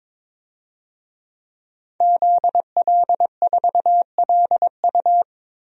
Morse code, heard as a keyed tone, about 22 wpm, 700 Hz. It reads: ZL4LU
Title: DXCC-100 Call Signs - 5 Characters - Repeated 3x - 22wpm